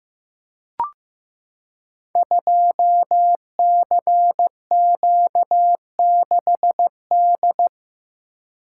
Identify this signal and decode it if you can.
Morse code, heard as a keyed tone, about 15 wpm, 700 Hz. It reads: 2CQ6D
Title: Code Groups - 5 Letter:Numbers - 15wpm